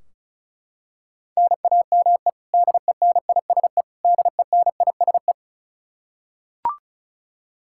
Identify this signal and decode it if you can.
Morse code, heard as a keyed tone, about 36 wpm, 700 Hz. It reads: NAME DENISE DENISE
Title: Practice QSO - Exchange 1 - Beginning - 35wpm